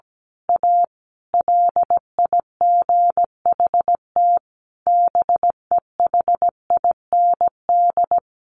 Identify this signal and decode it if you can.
Morse code, heard as a keyed tone, about 17 wpm, 700 Hz. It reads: A LIGHT BEHIND
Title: Sentences from Top 1000 Words 17wpm